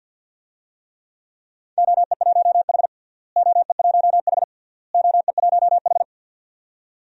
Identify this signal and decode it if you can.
Morse code, heard as a keyed tone, about 52 wpm, 700 Hz. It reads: OE1H OE1H OE1H
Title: DXCC-100 Call Signs - 4 Characters - Repeated 3x - 50wpm